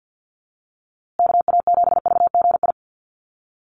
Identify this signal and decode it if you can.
Morse code, heard as a keyed tone, about 50 wpm, 700 Hz. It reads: KA7VGI
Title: USA Call Signs Repeated 2x 50wpm